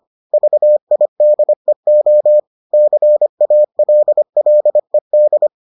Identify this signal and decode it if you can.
Morse code, heard as a keyed tone, about 25 wpm, 600 Hz. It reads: VIDEO CALLED